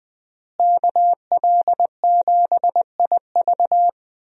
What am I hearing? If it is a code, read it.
Morse code, heard as a keyed tone, about 20 wpm, 700 Hz. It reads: KL7IV
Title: USA Call Signs - 5 Characters - Repeated 3x - 20wpm